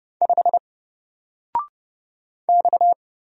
Morse code, heard as a keyed tone, about 30 wpm, 700 Hz. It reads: 5 X
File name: Single-Letter-Number-30wpm